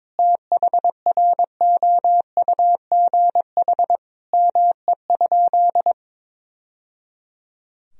Morse code, heard as a keyed tone, about 21 wpm, 700 Hz. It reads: THROUGH ME?
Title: Sentences from Top 1000 Words - Review Part 1 - 22wpm